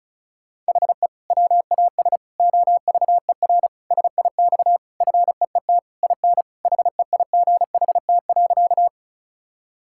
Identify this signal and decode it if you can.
Morse code, heard as a keyed tone, about 36 wpm, 700 Hz. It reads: HE WAS OVER SIX FEET IN HEIGHT.
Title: Sentences from Top 1600 Words 35wpm